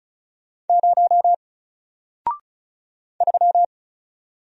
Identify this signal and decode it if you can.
Morse code, heard as a keyed tone, about 35 wpm, 700 Hz. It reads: 0 3
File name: Single-Letter-Number-35wpm